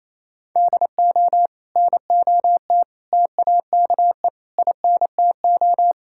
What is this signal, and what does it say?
Morse code, heard as a keyed tone, about 28 wpm, 700 Hz. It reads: DO NOT TAKE INTO
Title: Sentences from Top 1600 Words 28wpm